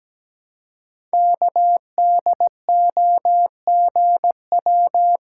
Morse code, heard as a keyed tone, about 17 wpm, 700 Hz. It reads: KDOGW
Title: Code Groups - 5 Letters - 17wpm